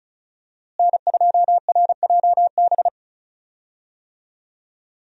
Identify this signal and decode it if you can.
Morse code, heard as a keyed tone, about 36 wpm, 700 Hz. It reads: N2RJB
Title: USA Call Signs - 5 Characters - Repeated 2x - 35wpm